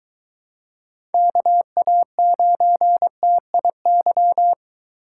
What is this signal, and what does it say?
Morse code, heard as a keyed tone, about 22 wpm, 700 Hz. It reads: KA9TIY